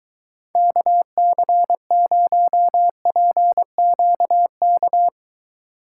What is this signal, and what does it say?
Morse code, heard as a keyed tone, about 22 wpm, 700 Hz. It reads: KC0PQK